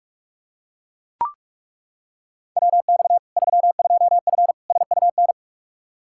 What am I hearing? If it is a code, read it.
Morse code, heard as a keyed tone, about 41 wpm, 700 Hz. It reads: WX 32F SUN
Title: Practice QSO - Exchange 2 - Info - 45wpm